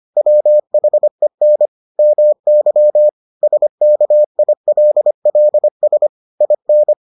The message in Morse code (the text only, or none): WHEN MY SKILLS IN